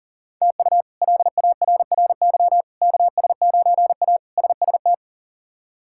Morse code, heard as a keyed tone, about 40 wpm, 700 Hz. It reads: TU LARRY KS9A SST
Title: K1USN SST QSO Practice - 40wpm